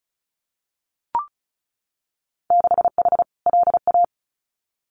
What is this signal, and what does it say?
Morse code, heard as a keyed tone, about 34 wpm, 700 Hz. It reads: 6H LA
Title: Winter Field Day Exchanges Encore 35wpm